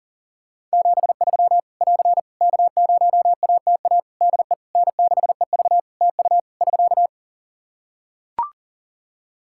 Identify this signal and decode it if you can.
Morse code, heard as a keyed tone, about 40 wpm, 700 Hz. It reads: 73 <AR> K0ATA DE N6EV TU <SK>
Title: Practice QSO - Exchange 4 - Ending - 40wpm